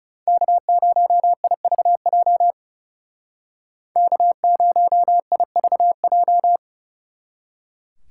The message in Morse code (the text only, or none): K0IVJ K0IVJ